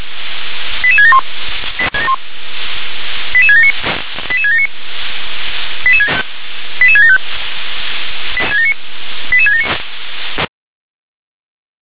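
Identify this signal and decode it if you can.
Five-tone selective call (ZVEI), heard as quick runs of five tones, tones of about 70 ms. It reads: 88671 88679 88679 88675 88679
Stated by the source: Five-tone ZVEI selective call received on SDR.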